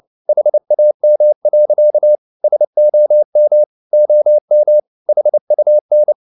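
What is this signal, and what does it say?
Morse code, heard as a keyed tone, about 27 wpm, 600 Hz. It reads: HAM. SOM OM HUN
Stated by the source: Fast (QRQ) Morse at 29 wpm, from CW med Gnister.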